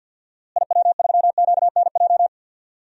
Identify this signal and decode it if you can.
Morse code, heard as a keyed tone, about 47 wpm, 700 Hz. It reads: IW3QNJ
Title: DXCC-100 Call Signs - 6 Characters - Repeated 3x - 50wpm